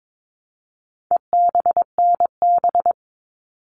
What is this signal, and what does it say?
Morse code, heard as a keyed tone, about 22 wpm, 700 Hz. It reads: EBNB
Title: All Cut Numbers - 4-Digit 22wpm